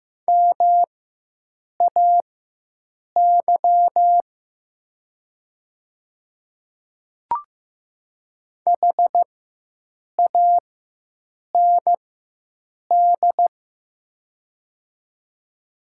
Morse code, heard as a keyed tone, about 15 wpm, spaced as for 7 wpm, 700 Hz. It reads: MAY HAND
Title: 4x Character Spacing - Top 200 Words - 15wpm